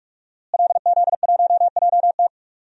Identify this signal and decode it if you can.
Morse code, heard as a keyed tone, about 46 wpm, 700 Hz. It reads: LZ1JT
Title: DXCC-100 Call Signs - 5 Characters - Repeated 3x - 45wpm